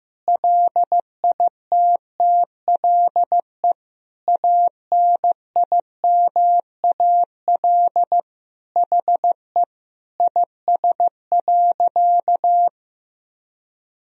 Morse code, heard as a keyed tone, about 15 wpm, 700 Hz. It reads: LITTLE ANIMAL HE IS.
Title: Sentences from Top 1000 Words - Review Part 1 - 15wpm